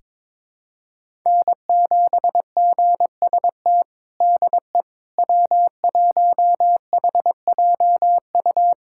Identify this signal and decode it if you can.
Morse code, heard as a keyed tone, about 22 wpm, 700 Hz. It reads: N7GST DE W1HJU
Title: USA Callsign DE Callsign - 5 Characters - 22wpm